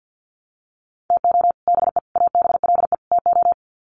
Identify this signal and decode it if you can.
Morse code, heard as a keyed tone, about 50 wpm, 700 Hz. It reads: TO BE ABLE TO
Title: Sets of 4 Words 50wpm